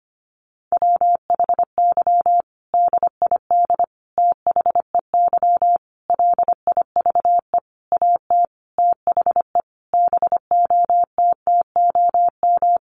Morse code, heard as a keyed tone, about 25 wpm, 700 Hz. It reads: WHY DID THEY LIVE AT THE BOTTOM
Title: Sentences from Top 900 Words 25wpm